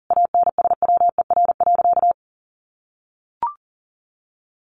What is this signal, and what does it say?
Morse code, heard as a keyed tone, about 40 wpm, 700 Hz. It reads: ANSWER.
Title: Sentences from Top 1500 Words 40wpm